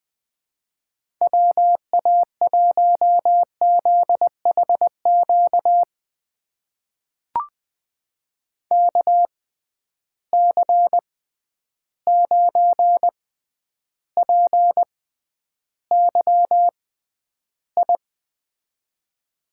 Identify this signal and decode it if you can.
Morse code, heard as a keyed tone, about 20 wpm, spaced as for 10 wpm, 700 Hz. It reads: WA1ZHQ KC9PYI
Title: USA Call Signs - 6 Characters - 6x Character Spacing - 20wpm